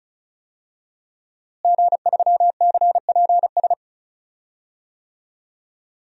Morse code, heard as a keyed tone, about 34 wpm, 700 Hz. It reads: G3CPS
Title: DXCC-100 Call Signs - 5 Characters - Repeated 2x - 35wpm